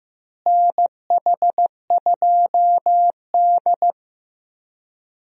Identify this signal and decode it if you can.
Morse code, heard as a keyed tone, about 15 wpm, 700 Hz. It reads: NH2D
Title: USA Call Signs - 4 Characters - Repeated 3x - 15wpm